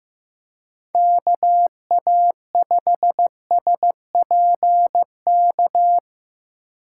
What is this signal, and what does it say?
Morse code, heard as a keyed tone, about 18 wpm, 700 Hz. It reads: KA5SPK
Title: Speed Racing - USA 2x3 Call Signs 15wpm